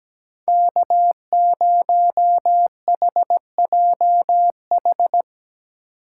Morse code, heard as a keyed tone, about 18 wpm, 700 Hz. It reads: K0HJH
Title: Speed Racing - USA 1x3 Call Signs 17wpm